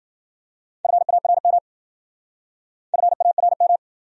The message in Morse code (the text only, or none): FARM FARM